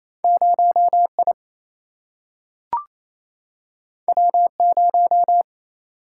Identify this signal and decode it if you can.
Morse code, heard as a keyed tone, about 28 wpm, 700 Hz. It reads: 0I W0
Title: Single-Letter Single-Number 28wpm